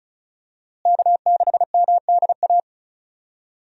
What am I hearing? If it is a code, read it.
Morse code, heard as a keyed tone, about 34 wpm, 700 Hz. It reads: K6MDA
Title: USA Call Signs - 5 Characters - Repeated 3x - 35wpm